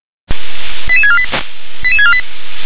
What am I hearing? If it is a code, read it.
Five-tone selective call (ZVEI), heard as quick runs of five tones, tones of about 70 ms. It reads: 88648 88648
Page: SDR/ZVEI